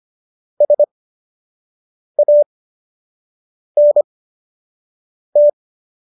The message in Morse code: SANT

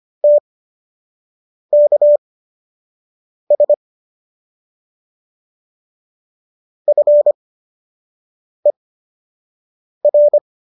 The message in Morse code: TKS FER

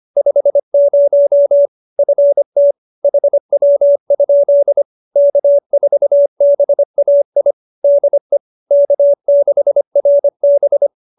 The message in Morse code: 50 FT HW? K4BAI DE K6RB

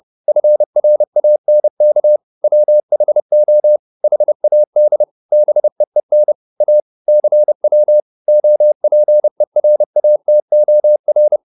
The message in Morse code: FRANK WHO HAD BEEN A CW OPERATOR

